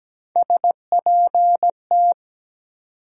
Morse code, heard as a keyed tone, about 17 wpm, 700 Hz. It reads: SPT
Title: USA Call Sign Postfixes - 3 Letters 17wpm